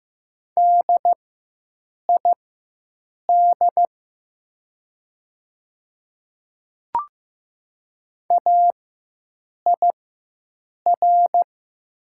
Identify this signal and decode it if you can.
Morse code, heard as a keyed tone, about 15 wpm, spaced as for 7 wpm, 700 Hz. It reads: DID AIR
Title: 4x Character Spacing - Top 300 Words - 15wpm